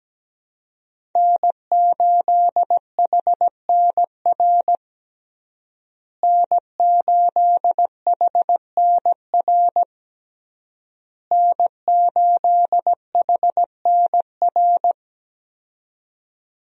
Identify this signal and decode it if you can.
Morse code, heard as a keyed tone, about 17 wpm, 700 Hz. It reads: N8HNR N8HNR N8HNR
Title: USA Call Signs - 5 Characters - Repeated 3x - 17wpm